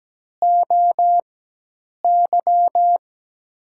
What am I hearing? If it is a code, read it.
Morse code, heard as a keyed tone, about 17 wpm, spaced as for 8 wpm, 700 Hz. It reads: OY
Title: USA Call Signs - 4x Character Spacing - 17wpm